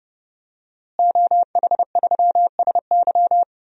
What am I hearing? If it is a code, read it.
Morse code, heard as a keyed tone, about 30 wpm, 700 Hz. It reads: OH3SY
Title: DXCC-100 Call Signs - 5 Characters - Repeated 2x - 30wpm